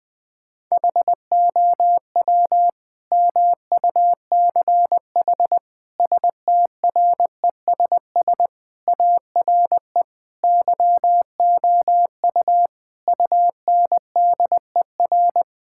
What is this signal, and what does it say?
Morse code, heard as a keyed tone, about 20 wpm, 700 Hz. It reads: HOW MUCH STRESS ARE YOU UNDER
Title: Sentences from Top 1000 Words - Review Part 6 - 20wpm